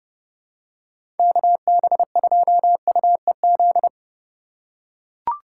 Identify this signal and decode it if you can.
Morse code, heard as a keyed tone, about 32 wpm, 700 Hz. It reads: KB2UEZ
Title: USA Call Signs - 6 Characters - 2x Character Spacing - 30wpm